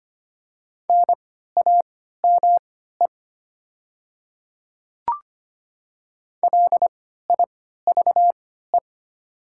Morse code, heard as a keyed tone, about 25 wpm, spaced as for 14 wpm, 690 Hz. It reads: NAME LIVE
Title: 3x Character Spacing - Top 200 Words - 25wpm